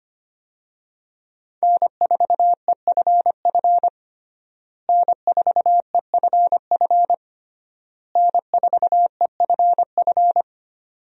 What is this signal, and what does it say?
Morse code, heard as a keyed tone, about 25 wpm, 700 Hz. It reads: N4EFF N4EFF N4EFF
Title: USA Call Signs - 5 Characters - Repeated 3x - 25wpm